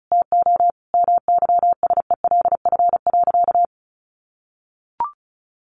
Morse code, heard as a keyed tone, about 34 wpm, 700 Hz. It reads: TO MYSELF.
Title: Sentences from Top 1300 Words 35wpm